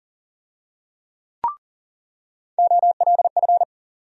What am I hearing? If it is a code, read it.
Morse code, heard as a keyed tone, about 40 wpm, 700 Hz. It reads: OLF
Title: USA Call Sign Postfixes - 3 Letters 40wpm